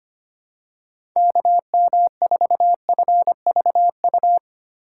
Morse code, heard as a keyed tone, about 25 wpm, 700 Hz. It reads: KM4FVU
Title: USA Call Signs - 6 Characters - 6x Character Spacing - 25wpm